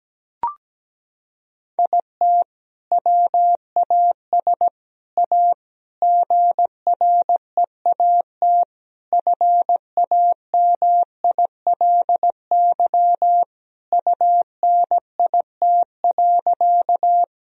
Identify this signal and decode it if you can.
Morse code, heard as a keyed tone, about 17 wpm, 700 Hz. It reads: IT WAS A GREAT FAMILY UNIT.
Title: Sentences from Top 1000 Words - Review Part 2 - 17wpm